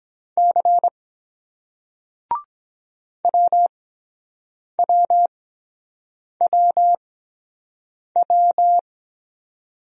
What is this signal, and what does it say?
Morse code, heard as a keyed tone, about 22 wpm, spaced as for 7 wpm, 700 Hz. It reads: C WWWW